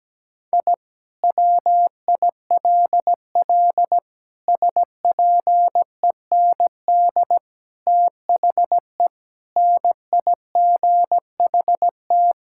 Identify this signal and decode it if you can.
Morse code, heard as a keyed tone, about 17 wpm, 700 Hz. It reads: I WILL SPEND THE NIGHT
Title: Sentences from Top 1000 Words - Review Part 5 - 17wpm